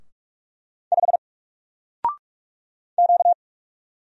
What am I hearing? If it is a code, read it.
Morse code, heard as a keyed tone, about 46 wpm, 700 Hz. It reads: 5 Q
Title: Single Letter-Number Alt 45wpm